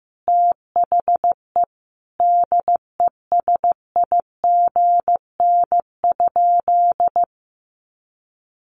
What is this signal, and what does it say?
Morse code, heard as a keyed tone, about 15 wpm, 700 Hz. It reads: THE DESIGN?
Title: Sentences from Top 800 Words 15wpm